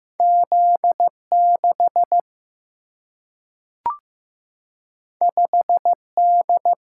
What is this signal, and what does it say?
Morse code, heard as a keyed tone, about 15 wpm, 700 Hz. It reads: Z 6 5 D
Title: Single-Letter Single-Number 15wpm